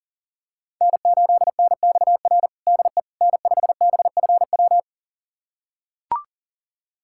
Morse code, heard as a keyed tone, about 40 wpm, 700 Hz. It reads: N8NXR DE N5BFW
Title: USA Callsign DE Callsign - 5 Characters - 40wpm